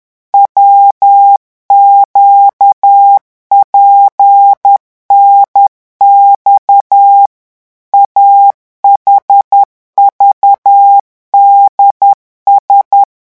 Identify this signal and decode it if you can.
Morse code, heard as a keyed tone, about 10 wpm, 800 Hz. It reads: WQPNX AHVDS